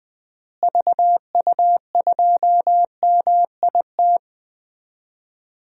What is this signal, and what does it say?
Morse code, heard as a keyed tone, about 20 wpm, 700 Hz. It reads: VU2MIT
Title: DXCC-100 Call Signs - 6 Characters - Repeated 3x - 20wpm